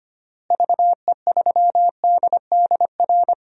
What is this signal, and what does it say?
Morse code, heard as a keyed tone, about 25 wpm, 700 Hz. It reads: VE3DDR
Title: DXCC-100 Call Signs - 6 Characters - Repeated 3x - 25wpm